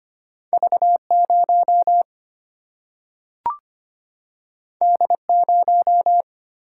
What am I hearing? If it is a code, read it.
Morse code, heard as a keyed tone, about 25 wpm, 700 Hz. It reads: V 0 D 0
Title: Single-Letter Single-Number 25wpm